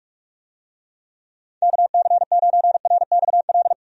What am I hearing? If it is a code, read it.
Morse code, heard as a keyed tone, about 43 wpm, 700 Hz. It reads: KC9RXL
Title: USA Call Signs - 6 Characters - 45wpm